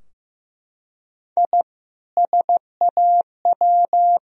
Morse code, heard as a keyed tone, about 15 wpm, 700 Hz. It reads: I SAW
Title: Sentences from Top 1000 Words - Review Part 3 - 15wpm